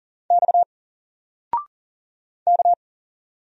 Morse code, heard as a keyed tone, about 40 wpm, 700 Hz. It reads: X K
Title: Single Letter-Number 40wpm